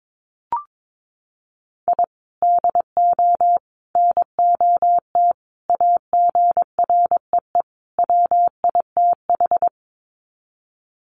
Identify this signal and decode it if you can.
Morse code, heard as a keyed tone, about 22 wpm, 700 Hz. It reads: I DO NOT AGREE WITH
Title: Sets of 5 Words Encore 22wpm